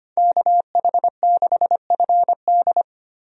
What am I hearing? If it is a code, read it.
Morse code, heard as a keyed tone, about 25 wpm, 690 Hz. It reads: KH6FD
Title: USA Call Signs - 5 Characters - 25wpm